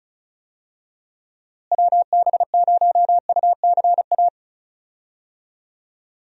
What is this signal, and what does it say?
Morse code, heard as a keyed tone, about 34 wpm, 700 Hz. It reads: WB0UCA
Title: USA Call Signs - 6 Characters - Repeated 2x - 35wpm